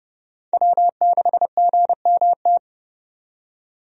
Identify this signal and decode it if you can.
Morse code, heard as a keyed tone, about 32 wpm, 700 Hz. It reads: W6GMT
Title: USA Call Signs - 5 Characters - 30wpm